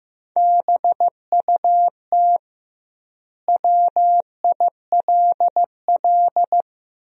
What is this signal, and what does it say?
Morse code, heard as a keyed tone, about 15 wpm, 700 Hz. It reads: BUT WILL
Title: 2x Word-Spacing - Sentences from Top 100 Words - 15wpm